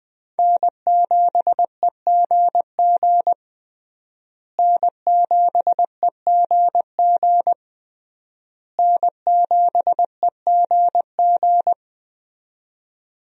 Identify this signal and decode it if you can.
Morse code, heard as a keyed tone, about 20 wpm, 700 Hz. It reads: N7EGG N7EGG N7EGG
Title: USA Call Signs - 5 Characters - Repeated 3x - 20wpm